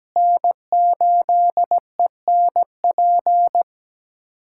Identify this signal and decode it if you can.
Morse code, heard as a keyed tone, about 17 wpm, 700 Hz. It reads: N8ENP
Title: USA Callsign DE Callsign - 5 Characters - 17wpm